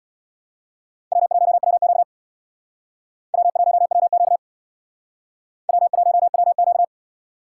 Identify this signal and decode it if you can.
Morse code, heard as a keyed tone, about 61 wpm, 700 Hz. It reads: W1WQ W1WQ W1WQ
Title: Speed Racing - USA 1x2 Call Signs 50wpm